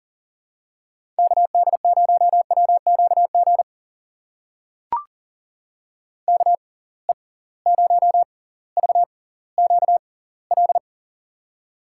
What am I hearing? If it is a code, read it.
Morse code, heard as a keyed tone, about 40 wpm, spaced as for 20 wpm, 700 Hz. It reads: KD0WQG KE0VQL
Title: USA Call Signs - 6x Character Spacing - 40wpm